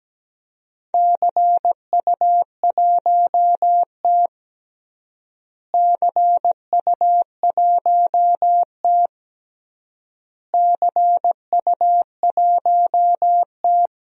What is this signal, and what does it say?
Morse code, heard as a keyed tone, about 17 wpm, 700 Hz. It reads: CU1T CU1T CU1T
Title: DXCC-100 Call Signs - 4 Characters - Repeated 3x - 17wpm